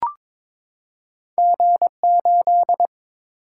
Morse code, heard as a keyed tone, about 22 wpm, 700 Hz. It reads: G8
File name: practice / 22wpm / Two Letter-Number Combinations - 22wpm